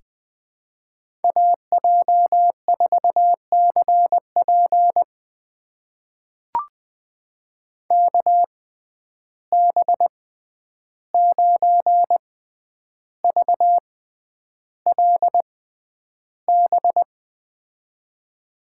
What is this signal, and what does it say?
Morse code, heard as a keyed tone, about 20 wpm, spaced as for 10 wpm, 700 Hz. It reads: AJ4CP KB9VLB
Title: USA Call Signs - 6x Character Spacing - 20wpm